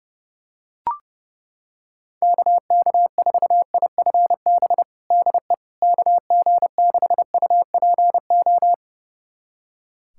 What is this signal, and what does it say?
Morse code, heard as a keyed tone, about 30 wpm, 700 Hz. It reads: KK4IFB DE KG6UPO
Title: USA Callsign DE Callsign - 6 Characters - 30wpm